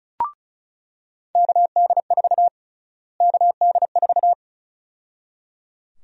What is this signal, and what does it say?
Morse code, heard as a keyed tone, about 34 wpm, 700 Hz. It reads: KD4 KD4
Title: USA Call Sign Prefixes - Repeated 2x - 35wpm